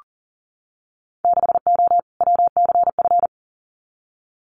Morse code, heard as a keyed tone, about 42 wpm, 700 Hz. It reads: 6O WCF
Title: Winter Field Day Exchanges Encore 40wpm